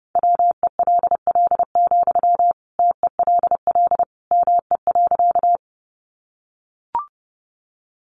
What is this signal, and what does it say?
Morse code, heard as a keyed tone, about 30 wpm, 700 Hz. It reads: WELL, TELL ME.
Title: Sentences from Top 200 Words 30wpm